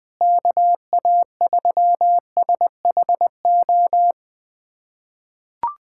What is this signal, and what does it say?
Morse code, heard as a keyed tone, about 21 wpm, 700 Hz. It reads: KA3SHO
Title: USA Callsign DE Callsign 20wpm